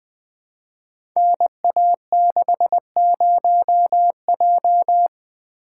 Morse code, heard as a keyed tone, about 20 wpm, 700 Hz. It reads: NA60J
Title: Code Groups - 5 Letter:Numbers - 20wpm